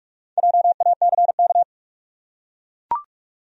Morse code, heard as a keyed tone, about 44 wpm, 700 Hz. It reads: JACK
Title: USA Common Names - 45wpm